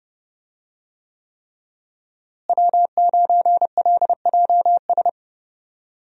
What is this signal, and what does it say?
Morse code, heard as a keyed tone, about 32 wpm, 700 Hz. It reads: W9LJS
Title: USA Call Signs - 5 Characters - Repeated 3x - 30wpm